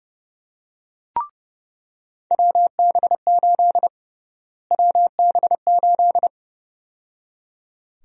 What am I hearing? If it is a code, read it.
Morse code, heard as a keyed tone, about 32 wpm, 700 Hz. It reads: WB8 WB8
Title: USA Call Sign Prefixes - Repeated 2x - 30wpm